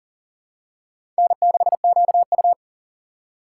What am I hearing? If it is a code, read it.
Morse code, heard as a keyed tone, about 40 wpm, 700 Hz. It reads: N6QU
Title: USA Call Signs - 4x Character Spacing - 40wpm